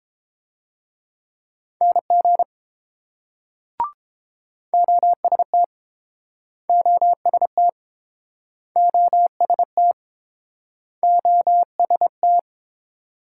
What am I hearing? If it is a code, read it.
Morse code, heard as a keyed tone, about 30 wpm, 700 Hz. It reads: NG OST OST OST OST